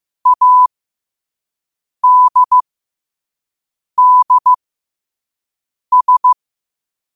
Morse code, heard as a keyed tone, about 15 wpm, spaced as for 5 wpm, 1000 Hz. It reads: ADDS
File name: qotdfiles / 5wpm